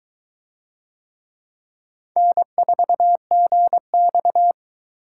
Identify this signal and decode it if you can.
Morse code, heard as a keyed tone, about 22 wpm, 700 Hz. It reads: N4GX